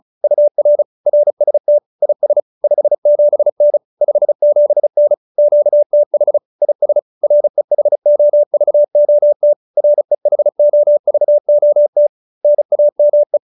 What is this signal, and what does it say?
Morse code, heard as a keyed tone, about 34 wpm, 590 Hz. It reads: UR RST IS 57N 57N QTH IS REHOVOT REHOVOT NAME
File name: QSO310 35